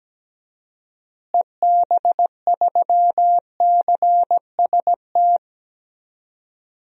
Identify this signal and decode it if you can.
Morse code, heard as a keyed tone, about 17 wpm, 700 Hz. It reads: EB3CST
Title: DXCC-100 Call Signs - 6 Characters - Repeated 2x - 17wpm